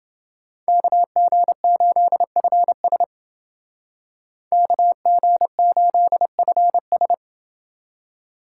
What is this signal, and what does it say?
Morse code, heard as a keyed tone, about 30 wpm, 700 Hz. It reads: KG8FS KG8FS